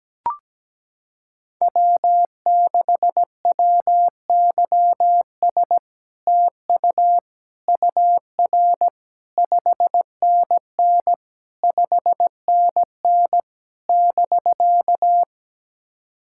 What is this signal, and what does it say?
Morse code, heard as a keyed tone, about 17 wpm, 700 Hz. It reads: W6WYS TU UR 5NN 5NN <BK>
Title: SOTA QSO Practice 17wpm